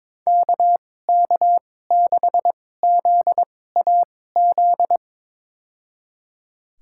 Morse code, heard as a keyed tone, about 22 wpm, spaced as for 16 wpm, 700 Hz. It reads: KK6ZAZ
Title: USA Call Signs - 6 Characters - 2x Character Spacing - 22wpm